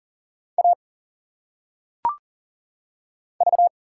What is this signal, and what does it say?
Morse code, heard as a keyed tone, about 40 wpm, 700 Hz. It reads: A V
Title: Single Letters - Phonetic Pronunciation 40wpm